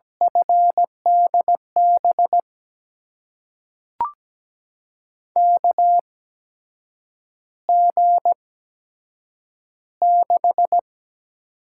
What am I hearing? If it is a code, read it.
Morse code, heard as a keyed tone, about 17 wpm, spaced as for 7 wpm, 700 Hz. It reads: FDB KG6
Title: USA Call Signs - 6 Characters - 8x Character Spacing - 17wpm